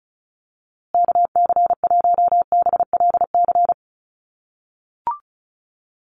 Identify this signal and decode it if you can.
Morse code, heard as a keyed tone, about 34 wpm, 700 Hz. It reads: KC1BLC
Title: USA Call Signs Repeated 3x 35wpm